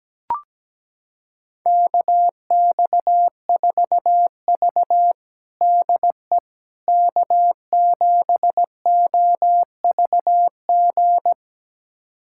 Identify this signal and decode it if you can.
Morse code, heard as a keyed tone, about 17 wpm, 700 Hz. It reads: KX4V DE K7OVG
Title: USA Callsign DE Callsign 17wpm